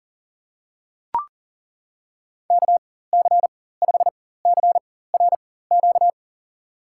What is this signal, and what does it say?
Morse code, heard as a keyed tone, about 40 wpm, spaced as for 19 wpm, 700 Hz. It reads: KC5CRQ
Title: USA Call Signs - 6 Characters - 4x Character Spacing - 40wpm